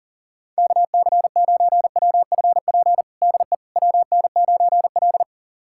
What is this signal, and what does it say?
Morse code, heard as a keyed tone, about 40 wpm, 700 Hz. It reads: KC9WFP DE WN9L
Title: USA Callsign DE Callsign 40wpm